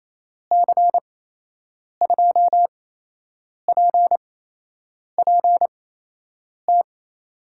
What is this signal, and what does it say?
Morse code, heard as a keyed tone, about 27 wpm, spaced as for 8 wpm, 700 Hz. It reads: C2PPT
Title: USA Call Signs - 6 Characters - 8x Character Spacing - 28wpm